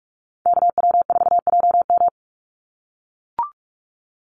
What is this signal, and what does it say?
Morse code, heard as a keyed tone, about 44 wpm, 700 Hz. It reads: KW4JM
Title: NA SOTA Activator Callsigns 45wpm